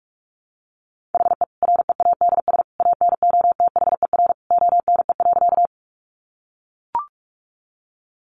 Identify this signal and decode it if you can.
Morse code, heard as a keyed tone, about 44 wpm, 700 Hz. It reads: HE READS ANOTHER ONE.
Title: Additional Easy Phrases 45wpm